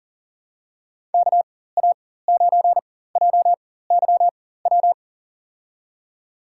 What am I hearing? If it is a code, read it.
Morse code, heard as a keyed tone, about 38 wpm, spaced as for 18 wpm, 700 Hz. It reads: KA9JYW
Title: USA Call Signs - 4x Character Spacing - 40wpm